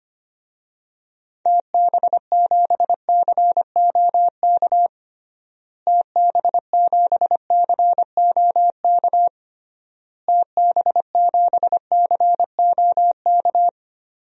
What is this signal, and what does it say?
Morse code, heard as a keyed tone, about 25 wpm, 700 Hz. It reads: TB7COK TB7COK TB7COK
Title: DXCC-100 Call Signs - 6 Characters - Repeated 3x - 25wpm